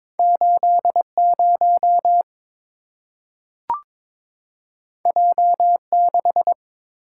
Morse code, heard as a keyed tone, about 22 wpm, 700 Hz. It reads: 80 J6
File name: practice / 22wpm / Two Letter-Number Combinations - 22wpm